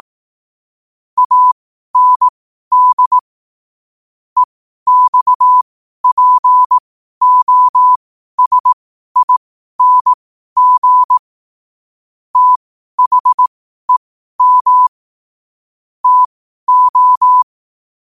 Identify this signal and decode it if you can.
Morse code, heard as a keyed tone, about 18 wpm, spaced as for 12 wpm, 1000 Hz. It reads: AND EXPOSING THEM TO
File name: qotdfiles / 12wpm